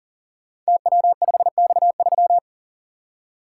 Morse code, heard as a keyed tone, about 40 wpm, 700 Hz. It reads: TW5X3
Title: Code Groups - 5 Letter:Numbers - 40wpm